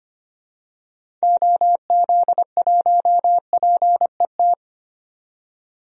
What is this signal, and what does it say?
Morse code, heard as a keyed tone, about 25 wpm, 700 Hz. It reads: OZ1PET
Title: DXCC-100 Call Signs - 6 Characters - 25wpm